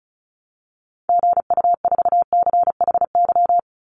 Morse code, heard as a keyed tone, about 34 wpm, 700 Hz. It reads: GU4CHY
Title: Call Signs Repeated 3x - 35wpm